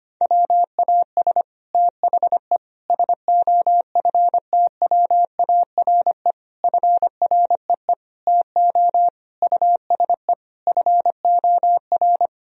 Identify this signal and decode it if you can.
Morse code, heard as a keyed tone, about 25 wpm, 700 Hz. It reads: WAS THE SOFTWARE FREE TO USE FOR
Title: Sentences from Top 1000 Words - Review Part 3 - 25wpm